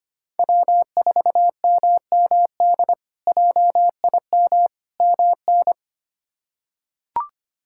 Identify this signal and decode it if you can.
Morse code, heard as a keyed tone, about 25 wpm, 700 Hz. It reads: W4MMD JIM MN
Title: K1USN SST QSO Practice - 25wpm